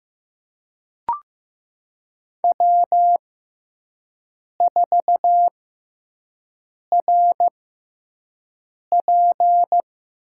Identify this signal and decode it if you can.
Morse code, heard as a keyed tone, about 16 wpm, spaced as for 5 wpm, 700 Hz. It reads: W4RP
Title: USA Call Signs - 6x Character Spacing - 15wpm